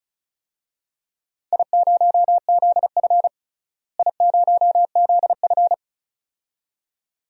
Morse code, heard as a keyed tone, about 36 wpm, 700 Hz. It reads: I0ZF I0ZF
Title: DXCC-100 Call Signs - 4 Characters - Repeated 2x - 35wpm